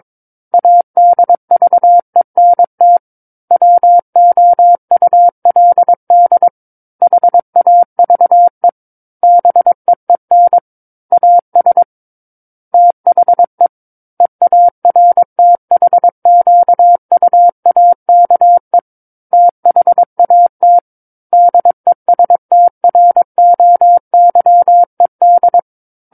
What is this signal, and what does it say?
Morse code, heard as a keyed tone, about 22 wpm, 700 Hz. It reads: ADVENT WOULD HAVE BEEN AS THE EARTHQUAKE THAT DESTROYED
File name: Book One / 13-Chapter 13 - 22 WPM